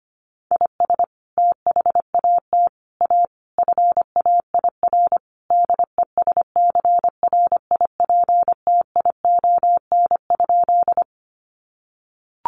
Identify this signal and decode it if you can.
Morse code, heard as a keyed tone, about 25 wpm, 700 Hz. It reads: IS THAT A FAIR DESCRIPTION?
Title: Sentences from Top 1300 Words 25wpm